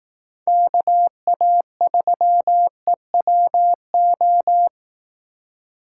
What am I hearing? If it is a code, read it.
Morse code, heard as a keyed tone, about 18 wpm, 700 Hz. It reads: KA3EWO
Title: Speed Racing - USA 2x3 Call Signs 15wpm